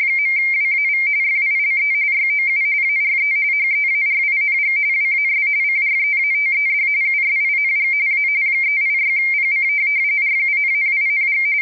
RTTY, about 45.45 baud. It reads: .WAV FILE YOU ARE HEARING WAS CREATED USING A 386PC RUNNING